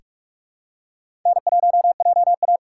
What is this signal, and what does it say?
Morse code, heard as a keyed tone, about 42 wpm, 700 Hz. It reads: N1JA
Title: USA Call Signs - 3x Character Spacing - 45wpm